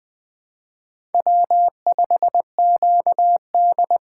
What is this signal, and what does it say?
Morse code, heard as a keyed tone, about 20 wpm, 700 Hz. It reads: W5QD